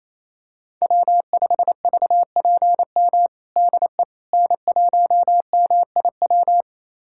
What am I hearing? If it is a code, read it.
Morse code, heard as a keyed tone, about 28 wpm, 700 Hz. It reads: W5VPM DE N1MIW
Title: USA Callsign DE Callsign - 5 Characters - 28wpm